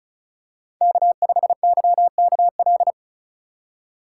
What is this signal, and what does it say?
Morse code, heard as a keyed tone, about 34 wpm, 700 Hz. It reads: K5YKL
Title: USA Call Signs - 5 Characters - 6x Character Spacing - 35wpm